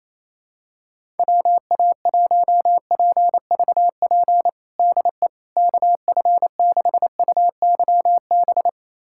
Morse code, heard as a keyed tone, about 28 wpm, 700 Hz. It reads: WA1PVP DE KF6UYB
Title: USA Callsign DE Callsign - 6 Characters - 28wpm